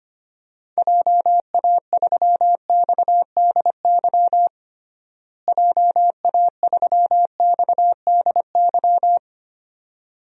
Morse code, heard as a keyed tone, about 25 wpm, 700 Hz. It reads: JA3XDY JA3XDY
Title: DXCC-100 Call Signs - 6 Characters - Repeated 2x - 25wpm